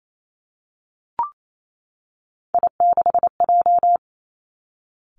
Morse code, heard as a keyed tone, about 28 wpm, 700 Hz. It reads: I6J
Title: Three Challenging Letter-Number Combinations 28wpm